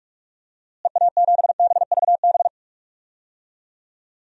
Morse code, heard as a keyed tone, about 42 wpm, 700 Hz. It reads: EA7BVB
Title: DXCC-100 Call Signs - 6 Characters - Repeated 2x - 45wpm